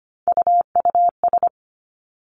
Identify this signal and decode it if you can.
Morse code, heard as a keyed tone, about 25 wpm, 700 Hz. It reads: UUS
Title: Three Challenging Letter-Number Combinations 25wpm